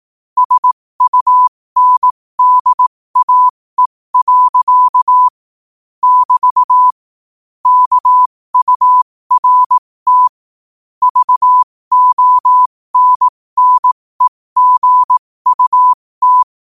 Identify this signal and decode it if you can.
Morse code, heard as a keyed tone, about 18 wpm, spaced as for 15 wpm, 1000 Hz. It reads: SUNDAE. <BT> KURT VONNEGUT
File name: qotdfiles / 15wpm